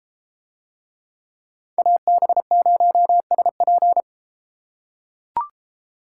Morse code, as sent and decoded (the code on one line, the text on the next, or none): .- -... ----- ... .--.
AB0SP